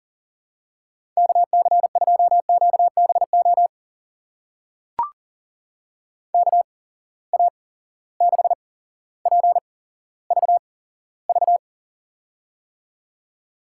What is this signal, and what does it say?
Morse code, heard as a keyed tone, about 37 wpm, spaced as for 17 wpm, 700 Hz. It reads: KC2QBO KA6PVV
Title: USA Call Signs - 6 Characters - 8x Character Spacing - 40wpm